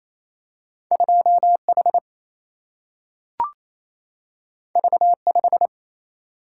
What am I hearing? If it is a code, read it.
Morse code, heard as a keyed tone, about 28 wpm, 700 Hz. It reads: 2H V5
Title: Single-Letter Single-Number 28wpm